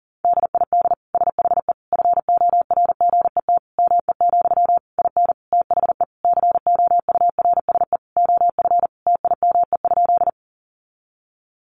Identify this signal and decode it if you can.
Morse code, heard as a keyed tone, about 40 wpm, 700 Hz. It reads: DID SHE FORGET ME, IN THE COURSE OF TIME?
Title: Sentences from Top 1200 Words 40wpm